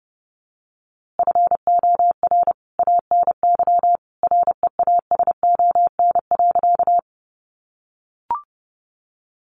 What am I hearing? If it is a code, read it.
Morse code, heard as a keyed tone, about 30 wpm, 700 Hz. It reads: FOR ANY REASON.
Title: A Sound Of Thunder 30wpm